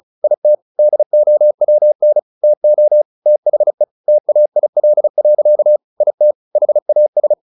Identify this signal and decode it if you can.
Morse code, heard as a keyed tone, about 34 wpm, 600 Hz. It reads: IT DOWN TO THE TAIL. IT HAS
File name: the old man and the sea 2 35wpm